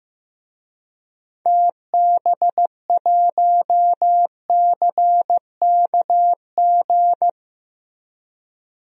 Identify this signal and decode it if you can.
Morse code, heard as a keyed tone, about 15 wpm, 700 Hz. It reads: TB1CKG
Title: DXCC-100 Call Signs - 6 Characters - Repeated 3x - 15wpm